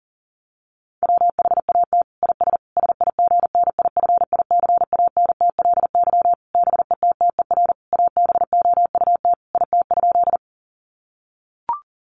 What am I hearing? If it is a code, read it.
Morse code, heard as a keyed tone, about 40 wpm, 700 Hz. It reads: WHAT IS SIGNIFICANTLY BETTER ABOUT IT?
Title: Sentences from Top 1500 Words 40wpm